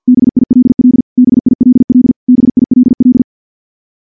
Morse code, heard as a keyed tone, about 46 wpm, 250 Hz. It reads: BILL BILL BILL
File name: morse / morse-training-50WPM